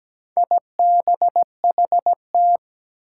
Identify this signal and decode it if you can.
Morse code, heard as a keyed tone, about 17 wpm, 700 Hz. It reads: IBHT
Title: Four Challenging Letter-Number Combinations 17wpm